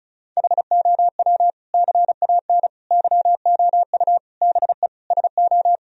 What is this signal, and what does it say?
Morse code, heard as a keyed tone, about 36 wpm, 700 Hz. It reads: HOW CAN YOU BE SO
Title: Sentences from Top 1000 Words - Review Part 1 - 35wpm